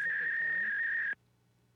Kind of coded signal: fax machine handshake